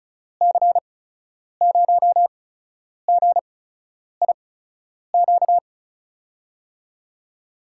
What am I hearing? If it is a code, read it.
Morse code, heard as a keyed tone, about 36 wpm, spaced as for 10 wpm, 700 Hz. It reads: C0GIQ
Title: USA Call Signs - 6 Characters - 8x Character Spacing - 35wpm